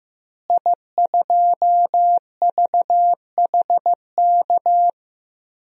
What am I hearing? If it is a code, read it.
Morse code, heard as a keyed tone, about 15 wpm, 700 Hz. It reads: I2VHK
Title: DXCC-100 Call Signs - 5 Characters - Repeated 3x - 15wpm